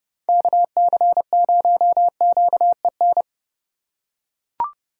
Morse code, heard as a keyed tone, about 27 wpm, 700 Hz. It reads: KC0QEN